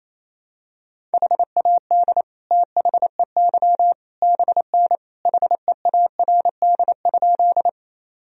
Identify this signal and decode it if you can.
Morse code, heard as a keyed tone, about 28 wpm, 700 Hz. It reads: HAD THEY BN HEARD?
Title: Sentences from Top 400 Words and Abbreviations 28wpm